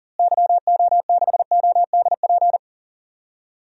Morse code, heard as a keyed tone, about 37 wpm, 700 Hz. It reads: YO6ODP
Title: DXCC-100 Call Signs - 6 Characters - Repeated 2x - 40wpm